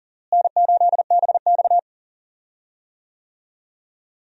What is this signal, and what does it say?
Morse code, heard as a keyed tone, about 40 wpm, 700 Hz. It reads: N8BX
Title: USA Call Signs - 4 Characters - Repeated 3x - 40wpm